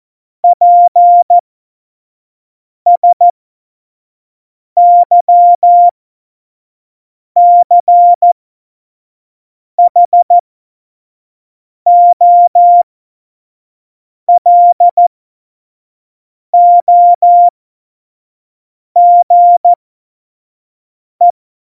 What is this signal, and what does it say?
Morse code, heard as a keyed tone, about 14 wpm, spaced as for 5 wpm, 700 Hz. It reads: PSYCHOLOGE
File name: Morsekurs / Witze / Witz 06 March 2018, 25 F70